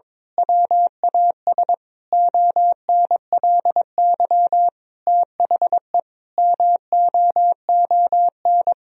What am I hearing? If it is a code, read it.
Morse code, heard as a keyed tone, about 22 wpm, 700 Hz. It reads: WAS ONLY THE MOON
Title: Sentences from Top 1000 Words - Review Part 3 - 22wpm